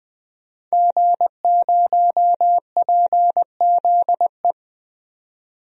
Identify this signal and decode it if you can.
Morse code, heard as a keyed tone, about 20 wpm, 700 Hz. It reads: G0PZE
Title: DXCC-100 Call Signs - 5 Characters - 20wpm